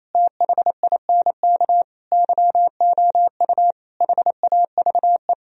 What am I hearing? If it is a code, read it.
Morse code, heard as a keyed tone, about 35 wpm, 700 Hz. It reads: THINK YOU HAVE